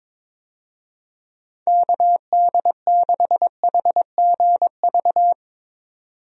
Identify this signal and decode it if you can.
Morse code, heard as a keyed tone, about 22 wpm, 700 Hz. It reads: KD6HGV
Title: USA Call Signs - 6 Characters - 22wpm